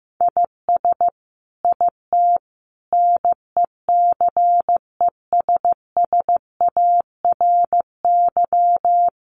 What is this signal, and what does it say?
Morse code, heard as a keyed tone, about 14 wpm, 700 Hz. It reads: IS IT NECESSARY
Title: Sentences from Top 1500 Words 15wpm